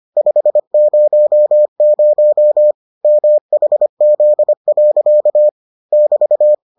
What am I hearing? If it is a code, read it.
Morse code, heard as a keyed tone, about 25 wpm, 600 Hz. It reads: 500 MHZ. <BT>